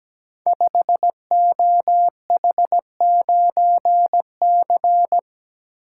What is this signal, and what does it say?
Morse code, heard as a keyed tone, about 17 wpm, 700 Hz. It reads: 5OH9C
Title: Code Groups - 5 Letter:Numbers - 17wpm